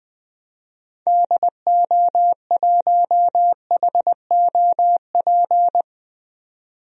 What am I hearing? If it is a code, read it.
Morse code, heard as a keyed tone, about 20 wpm, 700 Hz. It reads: DO1HOP
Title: DXCC-100 Call Signs - 6 Characters - 20wpm